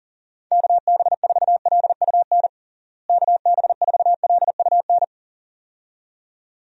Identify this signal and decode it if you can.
Morse code, heard as a keyed tone, about 37 wpm, 700 Hz. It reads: KB4LUN KB4LUN
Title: USA Call Signs - 6 Characters - Repeated 2x - 40wpm